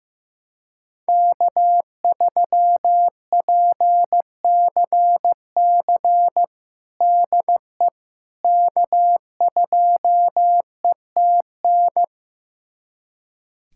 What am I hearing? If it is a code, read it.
Morse code, heard as a keyed tone, about 15 wpm, 700 Hz. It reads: K3PCC DE K2ETN
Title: USA Callsign DE Callsign - 5 Characters - 15wpm